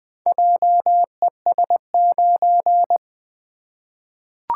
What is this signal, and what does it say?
Morse code, heard as a keyed tone, about 20 wpm, 700 Hz. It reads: JES9
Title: Four Letter-Number Combinations - 20wpm